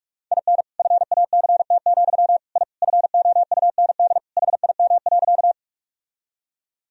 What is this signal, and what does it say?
Morse code, heard as a keyed tone, about 42 wpm, 700 Hz. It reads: IN FACT, I FOUND HIM.
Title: Sentences from Top 1000 Words - Review Part 3 - 45wpm